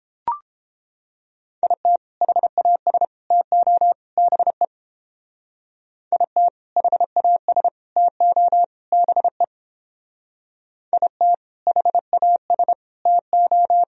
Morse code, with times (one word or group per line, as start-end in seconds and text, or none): 1.6-2.0 IT
2.2-3.0 HAS
3.3-3.9 TO
4.2-4.6 BE
6.1-6.5 IT
6.8-7.7 HAS
8.0-8.6 TO
8.9-9.4 BE
10.9-11.3 IT
11.7-12.7 HAS
13.1-13.8 TO